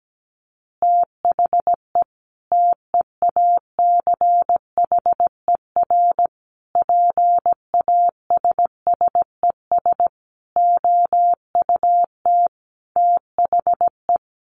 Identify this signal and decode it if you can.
Morse code, heard as a keyed tone, about 17 wpm, 700 Hz. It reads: THE TEACHER PASSES OUT THE
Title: Additional Easy Phrases 17wpm